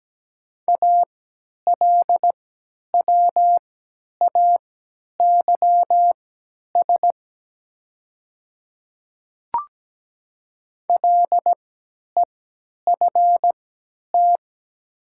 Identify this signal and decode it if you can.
Morse code, heard as a keyed tone, about 17 wpm, spaced as for 10 wpm, 690 Hz. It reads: ALWAYS LEFT
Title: 3x Character Spacing - Top 300 Words - 17wpm